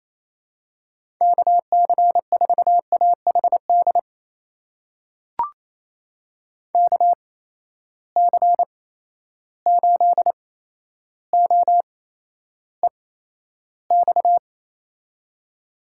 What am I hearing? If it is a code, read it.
Morse code, heard as a keyed tone, about 28 wpm, spaced as for 12 wpm, 700 Hz. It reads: KC4AHD KC8OEX
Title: USA Call Signs - 6 Characters - 8x Character Spacing - 28wpm